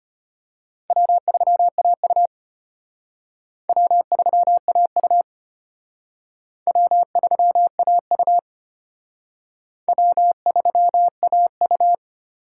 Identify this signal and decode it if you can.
Morse code, heard as a keyed tone, about 31 wpm, 700 Hz. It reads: W3AU W3AU W3AU W3AU
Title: Speed Racing - USA 1x2 Call Signs 25wpm